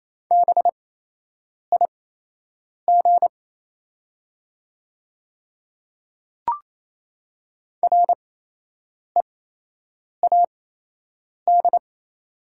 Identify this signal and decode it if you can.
Morse code, heard as a keyed tone, about 27 wpm, spaced as for 8 wpm, 690 Hz. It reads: BIG READ
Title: 8x Character Spacing - Top 300 Words - 28wpm